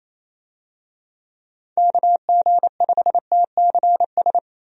Morse code, heard as a keyed tone, about 28 wpm, 700 Hz. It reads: KG5TCS
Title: USA Call Signs - 6 Characters - Repeated 2x - 28wpm